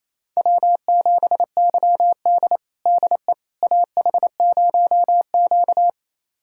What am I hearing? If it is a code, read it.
Morse code, heard as a keyed tone, about 28 wpm, 700 Hz. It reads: W7YD DE AH0Q
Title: USA Callsign DE Callsign - 4 Characters - 28wpm